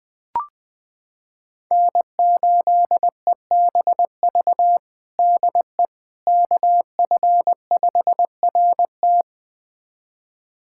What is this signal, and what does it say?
Morse code, heard as a keyed tone, about 21 wpm, 700 Hz. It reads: N8EBV DE KF5RT
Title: USA Callsign DE Callsign - 5 Characters - 20wpm